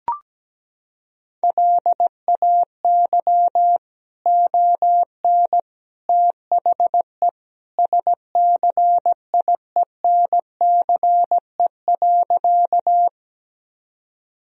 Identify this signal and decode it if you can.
Morse code, heard as a keyed tone, about 17 wpm, 700 Hz. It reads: LAY ON THE SCIENCE.
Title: Sentences from Top 1000 Words - Review Part 2 - 17wpm